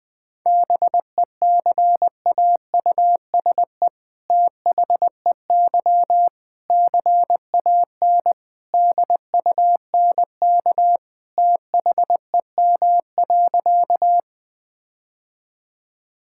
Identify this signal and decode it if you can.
Morse code, heard as a keyed tone, about 21 wpm, 700 Hz. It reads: BECAUSE THEY CAN DUNK THEM.
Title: All-Age Appropriate Jokes - 20wpm